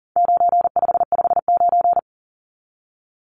Morse code, heard as a keyed tone, about 40 wpm, 700 Hz. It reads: 9559
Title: Numbers - 4-Digit 40wpm